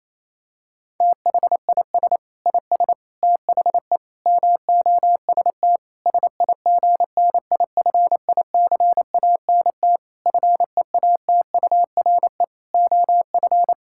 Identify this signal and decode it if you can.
Morse code, heard as a keyed tone, about 28 wpm, 700 Hz. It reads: THIS IS THE MOST SIGNIFICANT FEATURE OF
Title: Sentences from Top 1000 Words - Review Part 6 - 28wpm